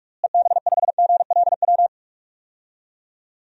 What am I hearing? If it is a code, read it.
Morse code, heard as a keyed tone, about 41 wpm, 700 Hz. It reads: EB5GLW
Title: DXCC-100 Call Signs - 6 Characters - Repeated 3x - 45wpm